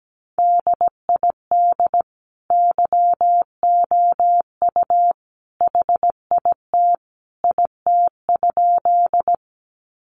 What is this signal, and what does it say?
Morse code, heard as a keyed tone, about 17 wpm, 700 Hz. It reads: DID YOU HIT IT?
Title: Sentences from Top 1200 Words 17wpm